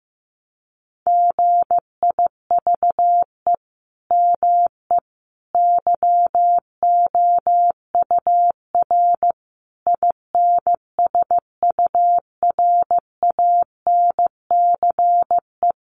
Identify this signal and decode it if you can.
Morse code, heard as a keyed tone, about 15 wpm, 700 Hz. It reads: GIVE ME YOUR INSURANCE
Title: Sentences from Top 1100 Words 15wpm